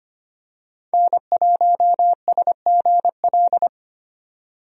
Morse code, heard as a keyed tone, about 25 wpm, 700 Hz. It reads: N1SGL
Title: USA Call Signs - 5 Characters - 25wpm